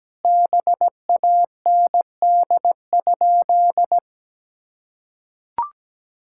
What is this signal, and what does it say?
Morse code, heard as a keyed tone, about 17 wpm, 700 Hz. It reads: BAND?
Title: Sentences from Top 1600 Words 17wpm